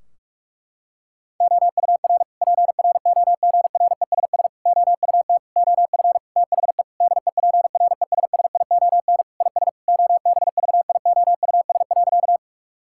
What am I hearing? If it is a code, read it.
Morse code, heard as a keyed tone, about 46 wpm, 700 Hz. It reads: OUR PROGRESS OUT OF THE DEPRESSION IS OBVIOUS.
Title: Sentences from Top 1600 Words 45wpm